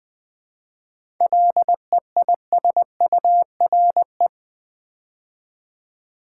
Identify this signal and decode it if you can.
Morse code, heard as a keyed tone, about 20 wpm, 700 Hz. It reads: LEISURE
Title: 7 Letter Words - First 3 Letters Given - 20wpm